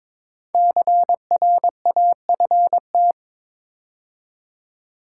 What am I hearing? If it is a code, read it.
Morse code, heard as a keyed tone, about 22 wpm, 700 Hz. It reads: CRAFT
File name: 5 Letter Words - First 3 Letters Given - 22wpm